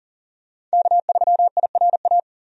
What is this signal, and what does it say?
Morse code, heard as a keyed tone, about 40 wpm, 700 Hz. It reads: K3IRA
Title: USA Call Signs - 5 Characters - 40wpm